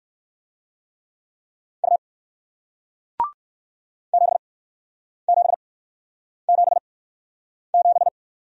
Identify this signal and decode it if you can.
Morse code, heard as a keyed tone, about 57 wpm, spaced as for 9 wpm, 700 Hz. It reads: U 7777